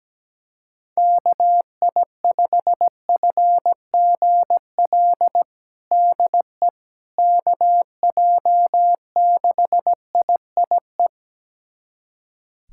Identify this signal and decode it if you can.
Morse code, heard as a keyed tone, about 17 wpm, 700 Hz. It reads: KI5FGL DE KJ6IIE
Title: USA Callsign DE Callsign - 6 Characters - 17wpm